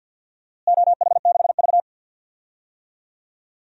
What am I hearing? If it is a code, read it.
Morse code, heard as a keyed tone, about 47 wpm, 700 Hz. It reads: OH6V
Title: DXCC-100 Call Signs - 4 Characters - Repeated 2x - 50wpm